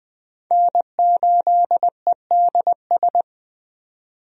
Morse code, heard as a keyed tone, about 20 wpm, 700 Hz. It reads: N8EDS
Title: USA Call Signs - 5 Characters - 20wpm